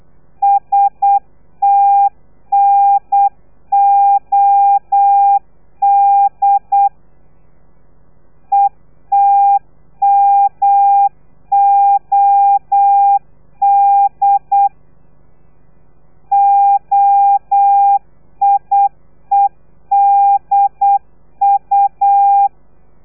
Morse code, heard as a keyed tone, about 8 wpm, 800 Hz. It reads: STNOD ETMOD OIEDU
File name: telegrafia / i2tfj / Lez3 35